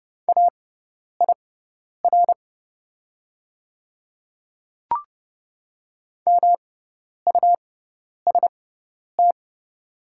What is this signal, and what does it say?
Morse code, heard as a keyed tone, about 30 wpm, spaced as for 10 wpm, 700 Hz. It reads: AIR MUST
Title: 6x Character Spacing - Top 200 Words - 30wpm